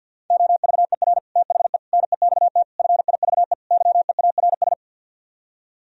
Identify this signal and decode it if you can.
Morse code, heard as a keyed tone, about 60 wpm, 700 Hz. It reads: OVER THE NEXT FIVE YEARS
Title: Speed-Racing - Sets of 5 Words 50wpm